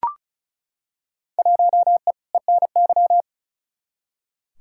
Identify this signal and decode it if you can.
Morse code, heard as a keyed tone, about 34 wpm, 700 Hz. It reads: 1E ENY
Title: Field Day Exchanges - 35wpm